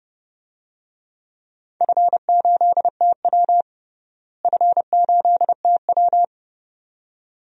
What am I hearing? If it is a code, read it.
Morse code, heard as a keyed tone, about 30 wpm, 700 Hz. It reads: F8TW F8TW
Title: DXCC-100 Call Signs - 4 Characters - Repeated 2x - 30wpm